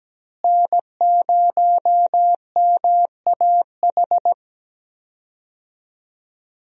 Morse code, heard as a keyed tone, about 17 wpm, 700 Hz. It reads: N0MAH
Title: USA Call Signs - 5 Characters - Repeated 3x - 17wpm